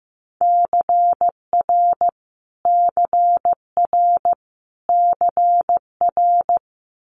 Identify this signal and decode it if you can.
Morse code, heard as a keyed tone, about 15 wpm, 700 Hz. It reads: CR CR CR
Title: Multi-letter Phonemes Repeated 3x 15wpm